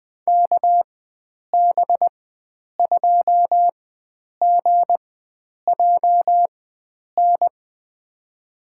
Morse code, heard as a keyed tone, about 20 wpm, spaced as for 9 wpm, 700 Hz. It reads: KB2GJN
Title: USA Call Signs - 6 Characters - 4x Character Spacing - 20wpm